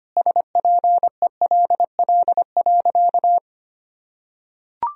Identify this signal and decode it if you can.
Morse code, heard as a keyed tone, about 25 wpm, 700 Hz. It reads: SPELL.
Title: Sentences from Top 1000 Words - Review Part 1 - 25wpm